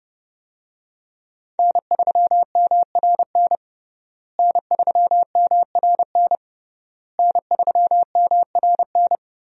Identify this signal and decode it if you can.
Morse code, heard as a keyed tone, about 30 wpm, 700 Hz. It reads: N3MRN N3MRN N3MRN
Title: USA Call Signs - 5 Characters - Repeated 3x - 30wpm